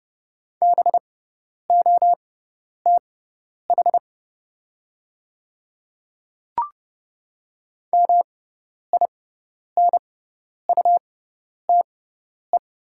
Morse code, heard as a keyed tone, about 30 wpm, spaced as for 10 wpm, 700 Hz. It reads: BOTH MINUTE
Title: 6x Character Spacing - Top 500 Words - 30wpm